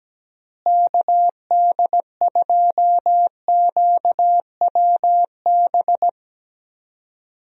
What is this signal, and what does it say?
Morse code, heard as a keyed tone, about 17 wpm, 700 Hz. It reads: KD2QWB
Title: USA Call Signs - 6 Characters - Repeated 3x - 17wpm